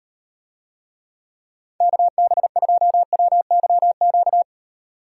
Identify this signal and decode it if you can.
Morse code, heard as a keyed tone, about 36 wpm, 700 Hz. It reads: KB2WYQ